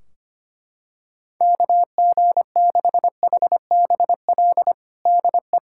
Morse code, heard as a keyed tone, about 25 wpm, 700 Hz. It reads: KG6HBL DE
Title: K1USN SST QSO Practice 25wpm